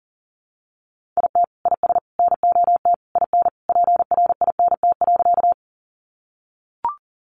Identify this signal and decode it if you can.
Morse code, heard as a keyed tone, about 40 wpm, 700 Hz. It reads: IT IS NOT IN PRINT.
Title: Sentences from Top 1400 Words 40wpm